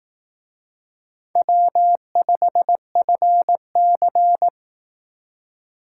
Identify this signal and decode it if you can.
Morse code, heard as a keyed tone, about 18 wpm, 700 Hz. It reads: W5FC
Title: Speed Racing - USA 1x2 Call Signs 15wpm